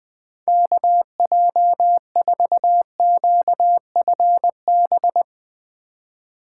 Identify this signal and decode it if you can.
Morse code, heard as a keyed tone, about 20 wpm, 700 Hz. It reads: KJ4QFB
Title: USA Call Signs - 6 Characters - 4x Character Spacing - 20wpm